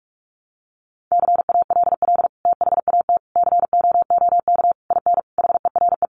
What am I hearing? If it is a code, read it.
Morse code, heard as a keyed tone, about 44 wpm, 700 Hz. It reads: CALL THAT COOK IN HERE
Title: Sentences from Top 1100 Words 45wpm